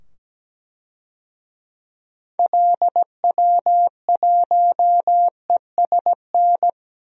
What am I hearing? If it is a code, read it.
Morse code, heard as a keyed tone, about 17 wpm, 700 Hz. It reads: LW1ESN
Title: DXCC-100 Call Signs - 6 Characters - 17wpm